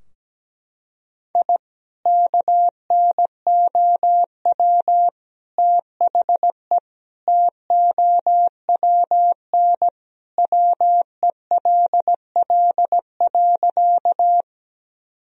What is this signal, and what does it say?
Morse code, heard as a keyed tone, about 17 wpm, 700 Hz. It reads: I KNOW THE TOWN WELL.
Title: Sentences from Top 1000 Words - Review Part 2 - 17wpm